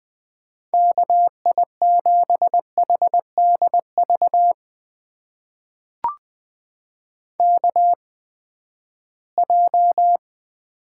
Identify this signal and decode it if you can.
Morse code, heard as a keyed tone, about 20 wpm, spaced as for 14 wpm, 700 Hz. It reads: KI7HDV KJ
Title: USA Call Signs - 6 Characters - 8x Character Spacing - 20wpm